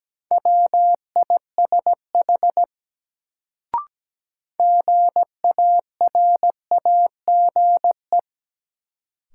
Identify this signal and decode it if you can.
Morse code, heard as a keyed tone, about 17 wpm, 700 Hz. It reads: WISH GARAGE
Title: Sentences from Top 1600 Words - New Word Drill Alt - 17wpm